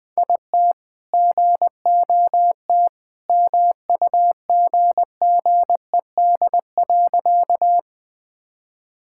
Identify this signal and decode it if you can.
Morse code, heard as a keyed tone, about 20 wpm, 700 Hz. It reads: IT GOT MUGGED.
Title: All-Age Appropriate Jokes - 20wpm